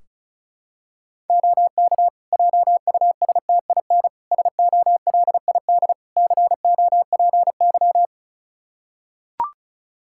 Morse code, heard as a keyed tone, about 34 wpm, 700 Hz. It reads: OK JUSTIN SOLID COPY
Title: Practice QSO - Exchange 3 - More Info - 35wpm